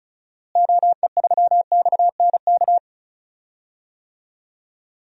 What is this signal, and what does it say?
Morse code, heard as a keyed tone, about 35 wpm, 700 Hz. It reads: OE3XNK
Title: DXCC-100 Call Signs - 6 Characters - Repeated 3x - 35wpm